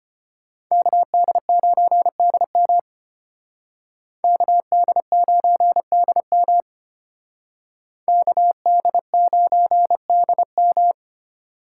Morse code, heard as a keyed tone, about 31 wpm, 700 Hz. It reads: KD9DM KD9DM KD9DM
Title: Speed Racing - USA 2x2 Call Signs 25wpm